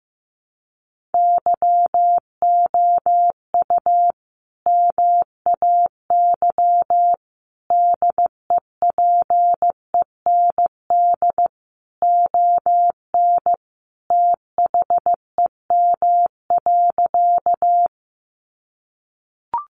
Morse code, heard as a keyed tone, about 14 wpm, 700 Hz. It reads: YOU MAY DEPEND ON THEM.
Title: Sentences from Top 1300 Words 15wpm